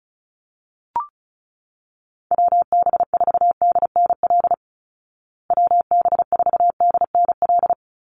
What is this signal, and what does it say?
Morse code, heard as a keyed tone, about 34 wpm, 700 Hz. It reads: WB4DNL WB4DNL
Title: Call Signs Repeated 2x - 35wpm